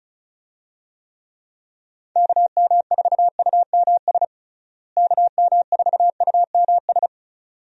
Morse code, heard as a keyed tone, about 36 wpm, 700 Hz. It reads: KM4UMS KM4UMS
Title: USA Call Signs - 6 Characters - Repeated 2x - 35wpm